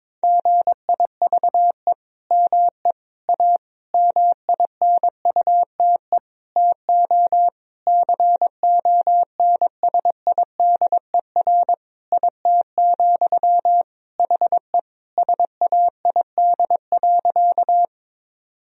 Morse code, heard as a keyed tone, about 22 wpm, 700 Hz. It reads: GIVE ME A MINUTE TO CONSIDER IT, HE SAID.
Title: Sentences from Top 1000 Words - Review Part 4 - 22wpm